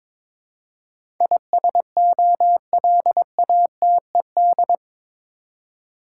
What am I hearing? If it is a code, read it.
Morse code, heard as a keyed tone, about 21 wpm, 700 Hz. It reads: ISOLATED
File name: practice / 22wpm / Eight Letter Words - 22wpm